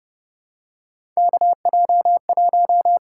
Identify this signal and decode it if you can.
Morse code, heard as a keyed tone, about 30 wpm, 700 Hz. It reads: KJ1
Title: USA Call Signs - 4 Characters - 6x Character Spacing - 30wpm